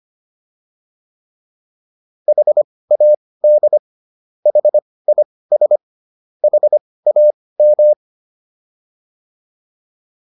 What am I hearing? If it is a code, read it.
Morse code, heard as a keyed tone, about 25 wpm, spaced as for 18 wpm, 600 Hz. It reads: HAD HIS HAM
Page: intermediate-practice-files